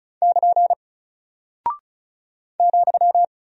Morse code, heard as a keyed tone, about 34 wpm, 700 Hz. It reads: <KN> ,
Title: Single Character (Letter:Number:Punctuation:Prosign) Alt 35wpm